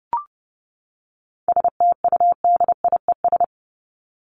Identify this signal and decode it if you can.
Morse code, heard as a keyed tone, about 30 wpm, 700 Hz. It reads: STUDIES
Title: ES:S Suffix - Top 100 Words with Base Words 30wpm